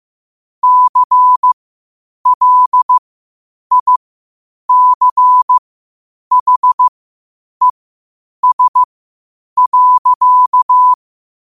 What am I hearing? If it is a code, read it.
Morse code, heard as a keyed tone, about 15 wpm, spaced as for 8 wpm, 1000 Hz. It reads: CLICHES.
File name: qotdfiles / 8wpm